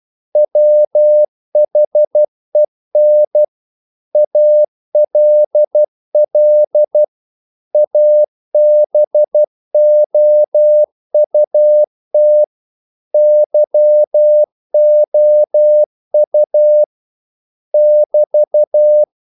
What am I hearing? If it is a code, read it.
Morse code, heard as a keyed tone, about 12 wpm, 600 Hz. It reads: WHEN ALL ABOUT YOU <BT>
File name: CW / Practice / If12